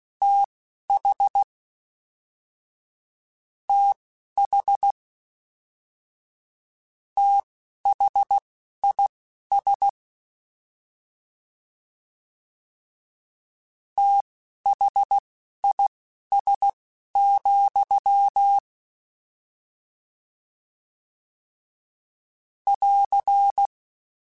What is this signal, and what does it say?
Morse code, heard as a keyed tone, about 16 wpm, spaced as for 12 wpm, 800 Hz. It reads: TH TH THIS THIS, <AR>
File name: Combinations / Combinations-mp3s / Combinations@10